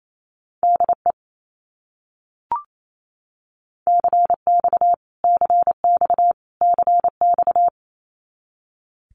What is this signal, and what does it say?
Morse code, heard as a keyed tone, about 28 wpm, 700 Hz. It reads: DE CX CX CX
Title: Abbreviations Repeated 3x - 28wpm